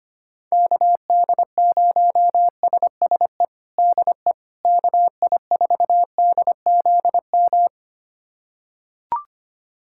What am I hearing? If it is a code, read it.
Morse code, heard as a keyed tone, about 25 wpm, 700 Hz. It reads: KD0SSE DE KI4DZM
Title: USA Callsign DE Callsign 25wpm